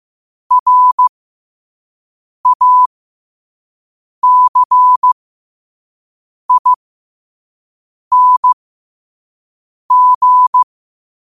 Morse code, heard as a keyed tone, about 15 wpm, spaced as for 5 wpm, 1000 Hz. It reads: RACING
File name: qotdfiles / 5wpm